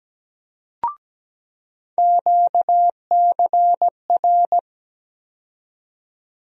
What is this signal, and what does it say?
Morse code, heard as a keyed tone, about 17 wpm, 700 Hz. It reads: QCR
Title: USA Call Sign Postfixes - 3 Letters 17wpm